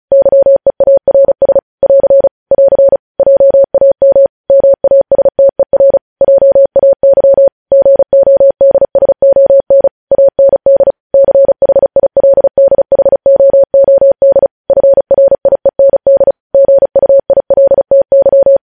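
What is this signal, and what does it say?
Morse code, heard as a keyed tone, about 36 wpm, 550 Hz. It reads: YEARS <AR> <AR> JAM MASTER JAY GODSON AND CHILDHOOD FRIEND GUILTY
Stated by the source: Morse code news headlines at 35.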